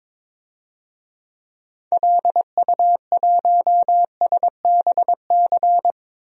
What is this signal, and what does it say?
Morse code, heard as a keyed tone, about 21 wpm, 700 Hz. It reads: LU1SBC
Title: DXCC-100 Call Signs - 6 Characters - 22wpm